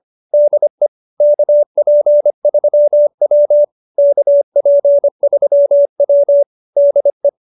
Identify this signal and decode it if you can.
Morse code, heard as a keyed tone, about 25 wpm, 570 Hz. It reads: DE KP3W KP3W DE